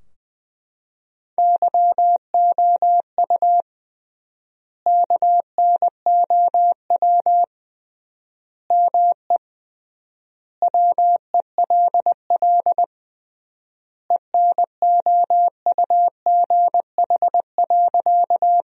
Morse code, heard as a keyed tone, about 20 wpm, 700 Hz. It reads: YOU KNOW ME WELL ENOUGH.
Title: 3x Word-Spacing - Sentences from Top 400 Words - 20wpm